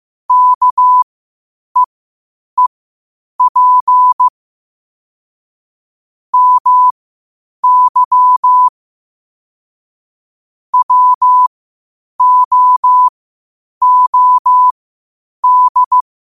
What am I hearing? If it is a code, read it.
Morse code, heard as a keyed tone, about 15 wpm, spaced as for 8 wpm, 1000 Hz. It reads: KEEP MY WOOD
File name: qotdfiles / 8wpm